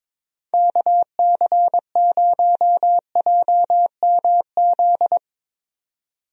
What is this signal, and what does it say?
Morse code, heard as a keyed tone, about 22 wpm, 700 Hz. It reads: KC0JMZ
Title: USA Call Signs - 6 Characters - 22wpm